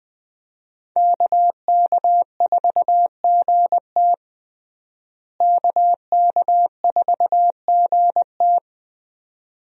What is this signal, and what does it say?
Morse code, heard as a keyed tone, about 20 wpm, 700 Hz. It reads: KK4GT KK4GT
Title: USA Call Signs - 5 Characters - Repeated 2x - 20wpm